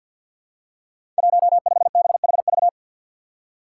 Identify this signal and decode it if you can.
Morse code, heard as a keyed tone, about 52 wpm, 700 Hz. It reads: 15BHV
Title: Five Challenging Letter-Number Combinations 50wpm